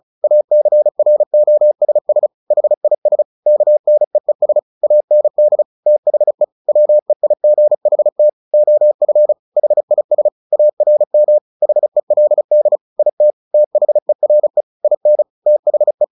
Morse code, heard as a keyed tone, about 34 wpm, 600 Hz. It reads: ACROSS HIS KNEES AND THE WEIGHT OF HIS ARM HELD IT THERE IN THE
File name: the old man and the sea / the old man and the sea 1 35wpm